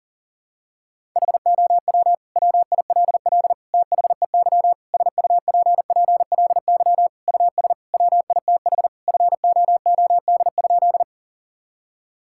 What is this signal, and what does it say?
Morse code, heard as a keyed tone, about 40 wpm, 700 Hz. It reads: HOW WILL THEY SUPPLY US WITH FOOD?
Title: Sentences from Top 1000 Words - Review Part 5 - 40wpm